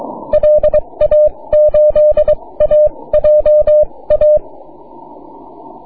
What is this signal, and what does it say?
Morse code, heard as a keyed tone, about 22 wpm, spaced as for 18 wpm, 590 Hz. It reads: LA8AJA
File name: wp / 9y4vu 6m cw